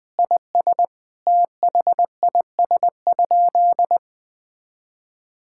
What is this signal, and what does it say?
Morse code, heard as a keyed tone, about 20 wpm, 700 Hz. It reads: IS THIS?
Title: Sentences from Top 1600 Words 20wpm